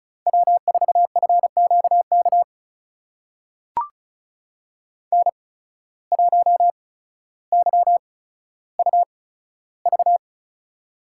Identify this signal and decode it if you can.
Morse code, heard as a keyed tone, about 36 wpm, spaced as for 15 wpm, 700 Hz. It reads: W4FQK N1YUV
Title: USA Call Signs - 5 Characters - 8x Character Spacing - 35wpm